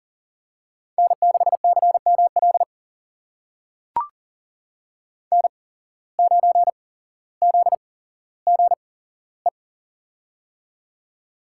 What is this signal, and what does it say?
Morse code, heard as a keyed tone, about 39 wpm, spaced as for 17 wpm, 700 Hz. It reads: N6CML N9ZGE
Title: USA Call Signs - 5 Characters - 8x Character Spacing - 40wpm